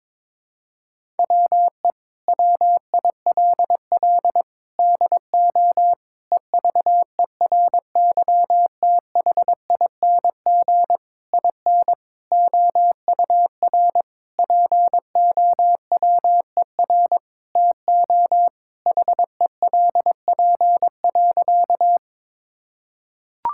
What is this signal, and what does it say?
Morse code, heard as a keyed tone, about 22 wpm, 700 Hz. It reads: WE WILL DO EVERYTHING IN OUR POWER TO HELP.
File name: practice / 22wpm / Sentences with 8-Word Phrases - 22wpm